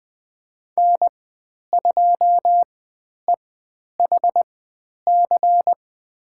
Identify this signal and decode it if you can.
Morse code, heard as a keyed tone, about 20 wpm, spaced as for 10 wpm, 700 Hz. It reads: N2EHC
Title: 102 Lesson 17g - Letter u - Call Signs 20wpm